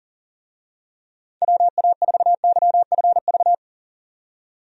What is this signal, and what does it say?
Morse code, heard as a keyed tone, about 40 wpm, 700 Hz. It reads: WA4YFV
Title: USA Call Signs - 6 Characters - 40wpm